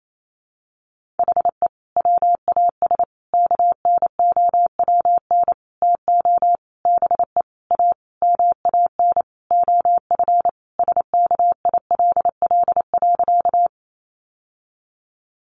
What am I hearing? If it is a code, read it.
Morse code, heard as a keyed tone, about 28 wpm, 700 Hz. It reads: HE WAS KNOWN TO BE A MAN OF SKILL.
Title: Sentences from Top 1200 Words 28wpm